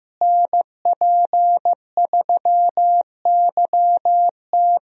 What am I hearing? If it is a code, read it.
Morse code, heard as a keyed tone, about 15 wpm, 700 Hz. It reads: NP3YT
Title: USA Call Signs - 5 Characters - Repeated 2x - 15wpm